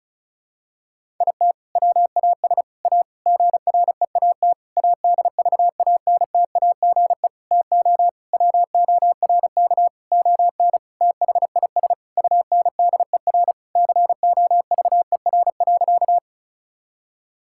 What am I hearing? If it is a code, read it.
Morse code, heard as a keyed tone, about 34 wpm, 700 Hz. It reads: IT WAS A GREAT ADVANTAGE TO WORK ON THIS UNDER COVER.
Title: Sentences from Top 1000 Words - Review Part 6 - 35wpm